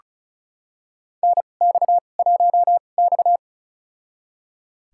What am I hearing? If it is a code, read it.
Morse code, heard as a keyed tone, about 34 wpm, spaced as for 25 wpm, 700 Hz. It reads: NX1X
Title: USA Call Signs - 4 Characters - 2x Character Spacing - 35wpm